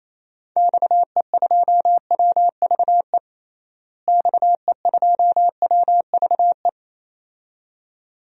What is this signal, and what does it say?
Morse code, heard as a keyed tone, about 28 wpm, 700 Hz. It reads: XE2WVE XE2WVE
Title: DXCC-100 Call Signs - 6 Characters - Repeated 2x - 28wpm